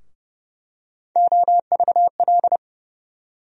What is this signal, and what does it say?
Morse code, heard as a keyed tone, about 31 wpm, 700 Hz. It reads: OVL
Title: USA Call Sign Postfixes - 3 Letters 30wpm